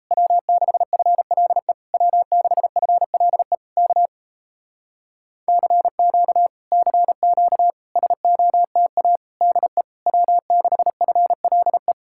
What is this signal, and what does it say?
Morse code, heard as a keyed tone, about 37 wpm, 700 Hz. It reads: W6FLE W6FLE K CQ CQ SOTA DE W6FLE